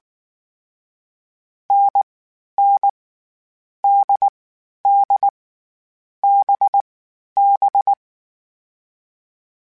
Morse code, heard as a keyed tone, about 19 wpm, 800 Hz. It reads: N N D D B B
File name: Difficult Combos / Difficult Combos-mp3s / NDB NDB6 @18